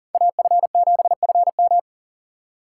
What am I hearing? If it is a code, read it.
Morse code, heard as a keyed tone, about 40 wpm, 700 Hz. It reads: AF7FM
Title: USA Call Signs - 3x Character Spacing - 40wpm